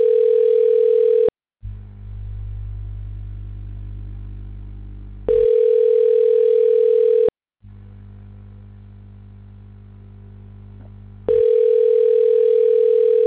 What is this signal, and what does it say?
Ringback tone (North America), 440 + 480 Hz.